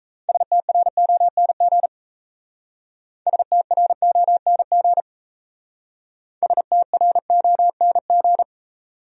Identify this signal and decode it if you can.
Morse code, heard as a keyed tone, about 35 wpm, 700 Hz. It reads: STRONG STRONG STRONG